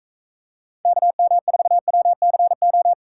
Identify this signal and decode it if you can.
Morse code, heard as a keyed tone, about 45 wpm, 700 Hz. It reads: KM4WCO